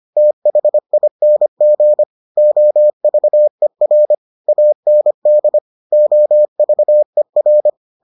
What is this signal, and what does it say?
Morse code, heard as a keyed tone, about 25 wpm, 600 Hz. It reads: THING OVER AND OVER